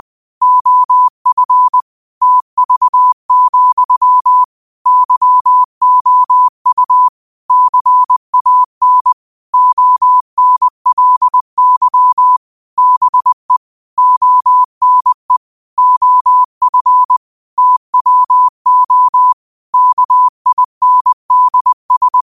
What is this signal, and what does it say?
Morse code, heard as a keyed tone, about 20 wpm, 1000 Hz. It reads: OF TV, YOU CAN ONLY BE ONE OF TWO KINDS
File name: qotdfiles / 20wpm